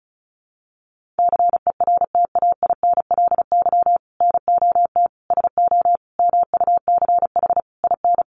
Morse code, heard as a keyed tone, about 34 wpm, 700 Hz. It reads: CERTAINLY NOT SO MUCH IN
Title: Sentences from Top 1500 Words 35wpm